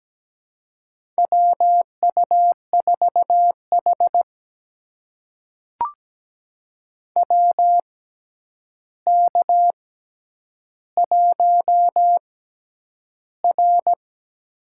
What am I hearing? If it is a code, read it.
Morse code, heard as a keyed tone, about 17 wpm, spaced as for 9 wpm, 700 Hz. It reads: WU4H WK1R
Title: USA Call Signs - 4 Characters - 6x Character Spacing - 17wpm